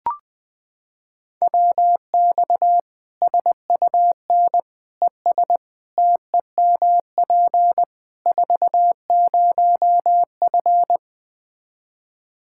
Morse code, heard as a keyed tone, about 20 wpm, 700 Hz. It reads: WX SUN ES TEMP 40F
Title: Practice QSO - Exchange 2 - Info - 20wpm